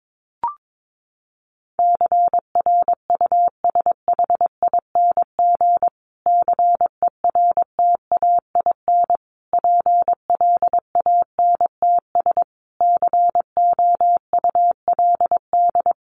Morse code, heard as a keyed tone, about 22 wpm, 700 Hz. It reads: CRUSHING CERTAIN PLANTS COULD
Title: A Sound Of Thunder 22wpm